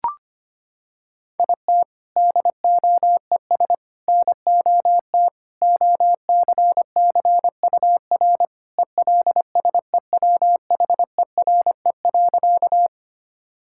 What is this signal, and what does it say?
Morse code, heard as a keyed tone, about 25 wpm, 700 Hz. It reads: IT DOES NOT OCCUR ELSEWHERE.
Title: Sentences from Top 1600 Words 25wpm